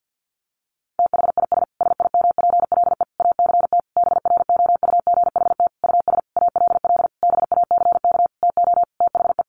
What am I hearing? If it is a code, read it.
Morse code, heard as a keyed tone, about 50 wpm, 700 Hz. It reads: THIS SIMPLE ACT BROUGHT US ALL BACK TO THE
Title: Sentences from Top 1500 Words 50wpm